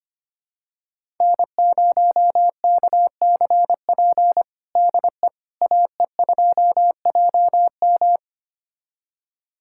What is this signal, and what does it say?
Morse code, heard as a keyed tone, about 25 wpm, 700 Hz. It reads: N0KCP DE AE2JM
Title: USA Callsign DE Callsign 25wpm